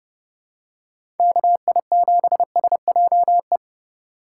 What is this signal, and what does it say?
Morse code, heard as a keyed tone, about 30 wpm, 700 Hz. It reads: KI7SJE
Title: USA Call Signs - 6 Characters - 4x Character Spacing - 30wpm